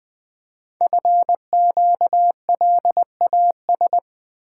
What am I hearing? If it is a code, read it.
Morse code, heard as a keyed tone, about 20 wpm, 700 Hz. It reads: FQLAS
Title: Code Groups - 5 Letters - 20wpm